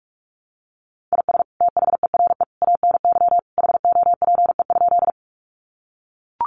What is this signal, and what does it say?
Morse code, heard as a keyed tone, about 44 wpm, 700 Hz. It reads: IS THERE ANY HOPE?
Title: Sentences from Top 1000 Words 45wpm